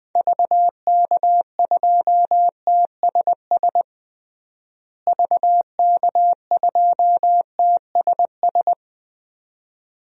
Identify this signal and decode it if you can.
Morse code, heard as a keyed tone, about 20 wpm, 700 Hz. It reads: VK2TSS VK2TSS
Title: DXCC-100 Call Signs - 6 Characters - Repeated 2x - 20wpm